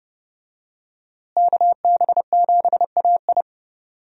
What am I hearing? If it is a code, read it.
Morse code, heard as a keyed tone, about 30 wpm, 700 Hz. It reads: KB7AI
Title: USA Call Signs - 5 Characters - Repeated 3x - 30wpm